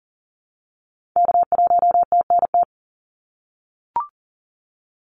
Morse code, heard as a keyed tone, about 40 wpm, 700 Hz. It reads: K1TNT
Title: NA SOTA Activator Callsigns 40wpm